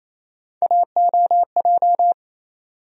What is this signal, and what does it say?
Morse code, heard as a keyed tone, about 27 wpm, 700 Hz. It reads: AOJ
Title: USA Call Sign Postfixes - 3 Letters 28wpm